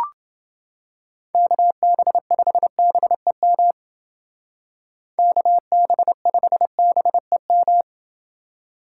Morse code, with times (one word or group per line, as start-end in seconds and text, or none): 1.3-3.7 KB5BEM
5.2-7.8 KB5BEM